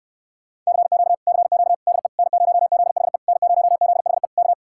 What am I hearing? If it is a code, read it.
Morse code, heard as a keyed tone, about 69 wpm, 700 Hz. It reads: CQ CQ DE N0ZHE N0ZHE K